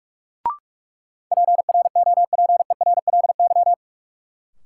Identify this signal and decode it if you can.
Morse code, heard as a keyed tone, about 46 wpm, 700 Hz. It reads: PROPERLY
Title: Sentences from Top 1100 Words - New Word Drill Alt - 45wpm